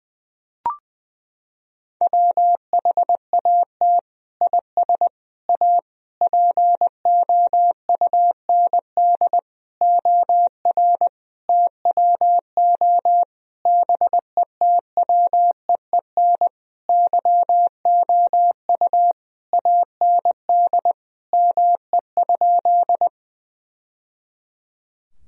Morse code, heard as a keyed tone, about 20 wpm, 700 Hz. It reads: WHAT IS A POUND OR TWO BETWEEN YOU AND ME?
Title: Sentences from Top 1000 Words - Review Part 2 - 20wpm